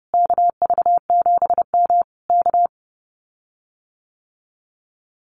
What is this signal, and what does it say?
Morse code, heard as a keyed tone, about 32 wpm, 700 Hz. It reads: KV7M K
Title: Common QSO Phrases Encore 30wpm